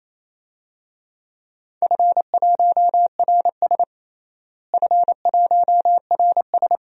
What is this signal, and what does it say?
Morse code, heard as a keyed tone, about 28 wpm, 700 Hz. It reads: F1RS F1RS
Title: DXCC-100 Call Signs - 4 Characters - Repeated 2x - 28wpm